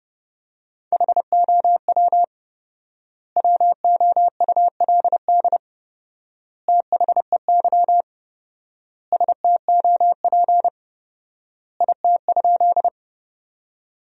Morse code, heard as a keyed tone, about 32 wpm, 700 Hz. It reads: HOW WOULD THEY STOP IT?
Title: 4x Word-Spacing - Sentences from Top 300 Words - 30wpm